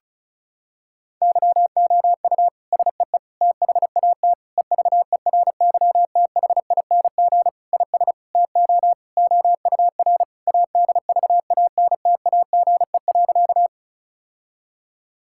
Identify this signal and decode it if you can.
Morse code, heard as a keyed tone, about 36 wpm, 700 Hz. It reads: YOU SEE THAT EVERYTHING IS TO OUR ADVANTAGE.
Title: Sentences from Top 1000 Words - Review Part 6 - 35wpm